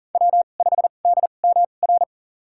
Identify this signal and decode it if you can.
Morse code, heard as a keyed tone, about 40 wpm, spaced as for 29 wpm, 700 Hz. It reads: W5DMR
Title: USA Call Signs - 2x Character Spacing - 40wpm